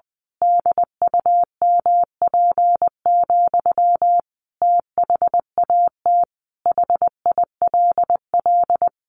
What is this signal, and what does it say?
Morse code, heard as a keyed tone, about 19 wpm, 700 Hz. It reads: DUMP, THAT HILL
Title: A Sound Of Thunder 20wpm